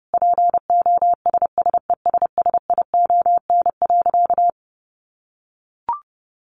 Morse code, heard as a keyed tone, about 30 wpm, 700 Hz. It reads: POSSESSION.
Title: Sentences from Top 1400 Words 30wpm